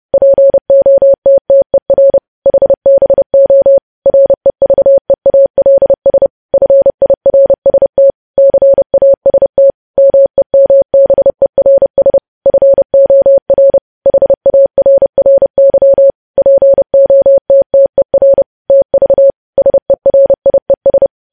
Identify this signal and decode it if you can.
Morse code, heard as a keyed tone, about 30 wpm, 550 Hz. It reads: POTTER HBO REVEALS FIRST CAST MEMBERS FOR HARRY POTTER TV SERIES
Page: news headlines in morse code at 30 wpm